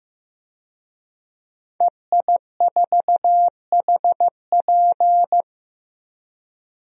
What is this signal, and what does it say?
Morse code, heard as a keyed tone, about 16 wpm, 700 Hz. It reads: EI4HP
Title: DXCC-100 Call Signs - 5 Characters - 15wpm